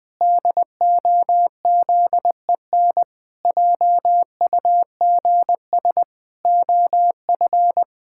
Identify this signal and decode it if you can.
Morse code, heard as a keyed tone, about 21 wpm, 700 Hz. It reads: DOZEN JUGS OF
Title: Pangram Challenge v2 - 20wpm